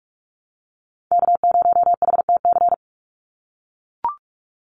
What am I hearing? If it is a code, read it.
Morse code, heard as a keyed tone, about 44 wpm, 700 Hz. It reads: K0HTC
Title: USA Call Signs 45wpm